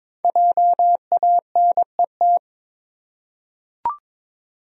Morse code, heard as a keyed tone, about 21 wpm, 700 Hz. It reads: JANET
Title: USA Common Names - 22wpm